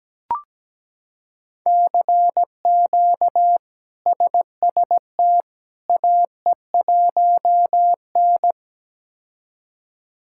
Morse code, heard as a keyed tone, about 17 wpm, 700 Hz. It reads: CQ SST AE1N
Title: K1USN SST QSO Practice - 17wpm